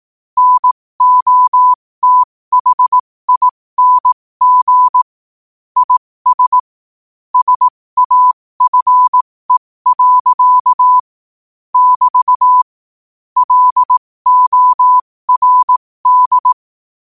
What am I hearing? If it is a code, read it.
Morse code, heard as a keyed tone, about 18 wpm, spaced as for 15 wpm, 1000 Hz. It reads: NOTHING IS SAFE. <BT> LORD
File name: qotdfiles / 15wpm